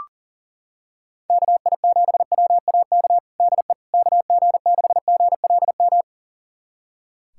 Morse code, heard as a keyed tone, about 40 wpm, 700 Hz. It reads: KI7WAK DE KG6GLM
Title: USA Callsign DE Callsign - 6 Characters - 40wpm